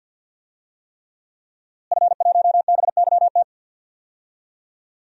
Morse code, heard as a keyed tone, about 51 wpm, 700 Hz. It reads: F1BYT
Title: DXCC-100 Call Signs - 5 Characters - Repeated 3x - 50wpm